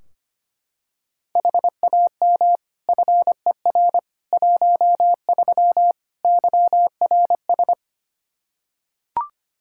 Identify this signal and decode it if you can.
Morse code, heard as a keyed tone, about 25 wpm, 700 Hz. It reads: HAM FER 13 YRS
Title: Practice QSO - Exchange 3 - More Info - 25wpm